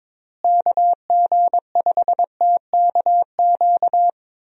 Morse code, heard as a keyed tone, about 22 wpm, 700 Hz. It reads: KG5TKQ
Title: USA Call Signs - 6 Characters - Repeated 2x - 22wpm